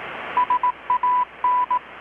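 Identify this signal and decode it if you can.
Morse code, heard as a keyed tone, about 18 wpm, 990 Hz. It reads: SAN
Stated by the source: Morse from KPH's Night of Nights XX maritime radiotelegraphy event, received in Chiapas, Mexico about 3,800 km away on a Kenwood R-600 with a grounded 10 m long-wire antenna.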